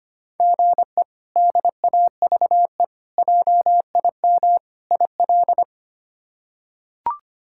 Morse code, heard as a keyed tone, about 25 wpm, 700 Hz. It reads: GE DAVE JIM IL
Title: K1USN SST QSO Practice - 25wpm